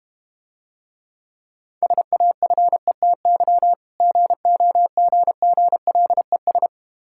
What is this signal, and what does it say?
Morse code, heard as a keyed tone, about 32 wpm, 700 Hz. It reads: SAFETY GOGGLES